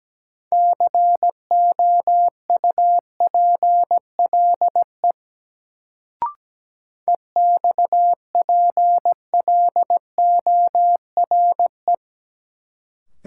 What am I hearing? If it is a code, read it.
Morse code, heard as a keyed tone, about 17 wpm, 700 Hz. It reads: COUPLE EXPLORE
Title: Sentences from Top 1100 Words - New Word Drill Alt - 17wpm